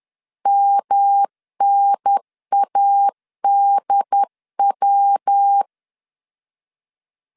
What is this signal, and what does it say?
Morse code, heard as a keyed tone, about 10 wpm, 800 Hz. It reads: MNADW
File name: CW / DET / 1 3 1500C S C 10wpm 3335